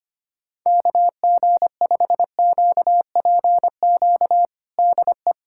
Morse code, heard as a keyed tone, about 25 wpm, 700 Hz. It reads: KG5QPQ DE
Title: USA Callsign DE Callsign - 6 Characters - 25wpm